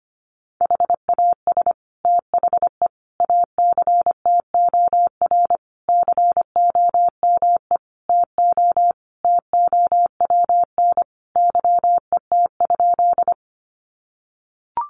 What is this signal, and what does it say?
Morse code, heard as a keyed tone, about 25 wpm, 700 Hz. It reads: HAS THE ACTOR COME TO TOWN YET?
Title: Sentences from Top 1500 Words 25wpm